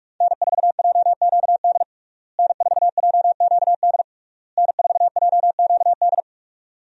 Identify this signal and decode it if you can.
Morse code, heard as a keyed tone, about 42 wpm, 700 Hz. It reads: N4JQD N4JQD N4JQD
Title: USA Call Signs - 5 Characters - Repeated 3x - 45wpm